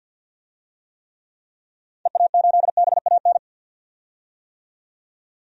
Morse code, heard as a keyed tone, about 48 wpm, 700 Hz. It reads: EA8BAN
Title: DXCC-100 Call Signs - 6 Characters - 50wpm